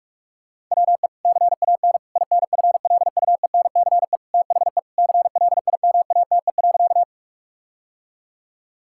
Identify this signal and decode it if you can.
Morse code, heard as a keyed tone, about 41 wpm, 700 Hz. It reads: WE CAN INFLUENCE THE CLIMATE.
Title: Sentences from Top 1600 Words 45wpm